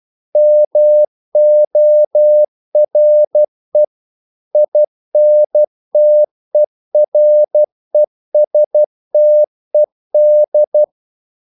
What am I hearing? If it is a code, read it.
Morse code, heard as a keyed tone, about 12 wpm, 600 Hz. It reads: MORE INTERESTED
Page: English quotations in CW (telegraphy - morse)